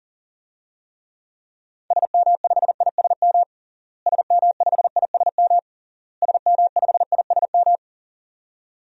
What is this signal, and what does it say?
Morse code, heard as a keyed tone, about 40 wpm, 700 Hz. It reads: SM5ISM SM5ISM SM5ISM
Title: DXCC-100 Call Signs - 6 Characters - Repeated 3x - 40wpm